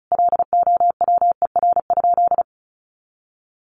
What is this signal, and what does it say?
Morse code, heard as a keyed tone, about 34 wpm, 700 Hz. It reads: LOWER?
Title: Sentences from Top 700 Words 35wpm